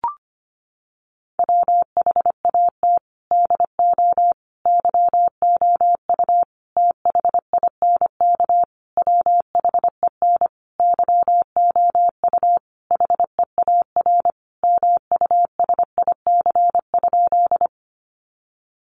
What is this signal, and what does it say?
Morse code, heard as a keyed tone, about 25 wpm, 700 Hz. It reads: WHAT DO YOU THINK WHEN YOU HEAR MUSIC?
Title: Sentences from Top 300 Words 25wpm